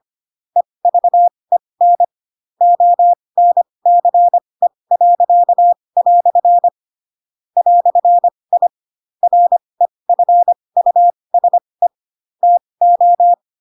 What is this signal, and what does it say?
Morse code, heard as a keyed tone, about 25 wpm, spaced as for 20 wpm, 700 Hz. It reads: EVEN ONCE." "I REFUSE TO
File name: Pinocchio / Pinocchio ch4 15 25 ik1hge